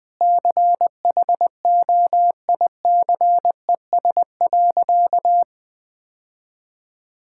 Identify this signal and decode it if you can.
Morse code, heard as a keyed tone, about 20 wpm, 700 Hz. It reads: CHOICES.
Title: Sentences with 3-Word Phrases - Spoken Beginning - 20wpm